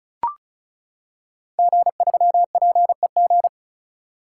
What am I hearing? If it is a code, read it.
Morse code, heard as a keyed tone, about 36 wpm, 690 Hz. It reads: G3PEG
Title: Code Groups - 5 Letter:Numbers - 35wpm